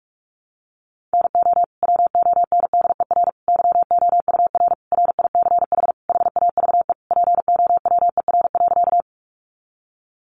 Morse code, heard as a keyed tone, about 44 wpm, 700 Hz. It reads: NO WONDER YOUR RICH HAVE POWER.
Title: Sentences from Top 1100 Words 45wpm